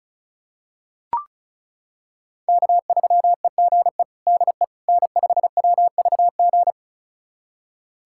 Morse code, heard as a keyed tone, about 36 wpm, 700 Hz. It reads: K3EGE DE N5WVG
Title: USA Callsign DE Callsign - 5 Characters - 35wpm